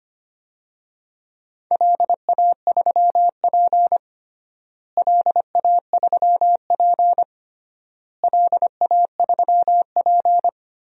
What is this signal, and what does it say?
Morse code, heard as a keyed tone, about 25 wpm, 700 Hz. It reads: LA3P LA3P LA3P
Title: DXCC-100 Call Signs - 4 Characters - Repeated 3x - 25wpm